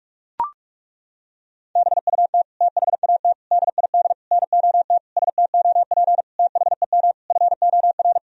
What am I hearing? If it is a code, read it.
Morse code, heard as a keyed tone, about 41 wpm, 700 Hz. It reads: BUT THAT DID NOT STOP THEM FOR
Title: Sentences from Top 1000 Words - Review Part 1 - 45wpm